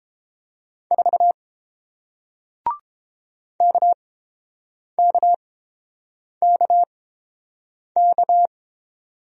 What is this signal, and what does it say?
Morse code, heard as a keyed tone, about 29 wpm, spaced as for 7 wpm, 700 Hz. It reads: 4 KKKK